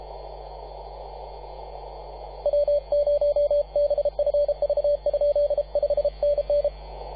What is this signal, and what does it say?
Morse code, heard as a keyed tone, about 33 wpm, 580 Hz, with radio noise. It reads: W0BFV? 5NN
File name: posts / 2015-sept-24-2118z-t2gc-18-071-cw